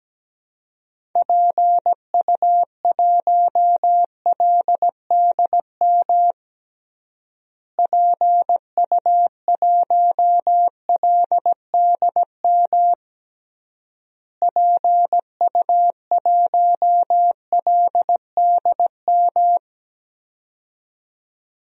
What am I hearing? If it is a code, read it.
Morse code, heard as a keyed tone, about 17 wpm, 700 Hz. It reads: PU1LDM PU1LDM PU1LDM
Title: DXCC-100 Call Signs - 6 Characters - Repeated 3x - 17wpm